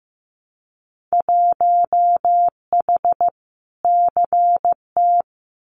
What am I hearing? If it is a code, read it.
Morse code, heard as a keyed tone, about 15 wpm, 700 Hz. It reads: 1H CT
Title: Winter Field Day Exchanges 15wpm